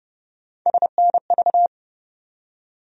Morse code, heard as a keyed tone, about 30 wpm, 700 Hz. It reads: SNV
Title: USA Call Sign Postfixes - 3 Letters 30wpm